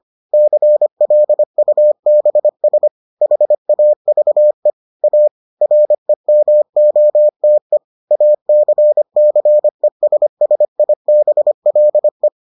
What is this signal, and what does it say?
Morse code, heard as a keyed tone, about 25 wpm, 600 Hz. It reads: CLUBS HAVE A REMOTE ACCESSIBLE